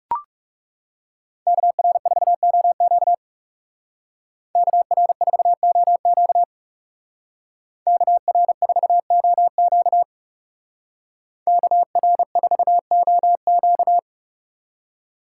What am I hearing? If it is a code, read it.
Morse code, heard as a keyed tone, about 36 wpm, 700 Hz. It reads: KR4OQ KR4OQ KR4OQ KR4OQ
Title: Speed Racing - USA 2x2 Call Signs 30wpm